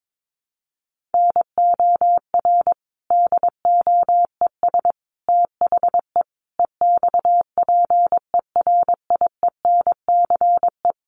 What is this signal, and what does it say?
Morse code, heard as a keyed tone, about 22 wpm, 700 Hz. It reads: NOR DOES THE EXPERIENCE
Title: Sentences from Top 1500 Words 22wpm